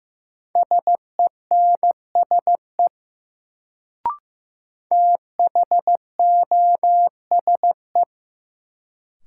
Morse code, heard as a keyed tone, about 15 wpm, 700 Hz. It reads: SENSE THOSE
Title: Sets of 2 Words - New Word Drill - 15wpm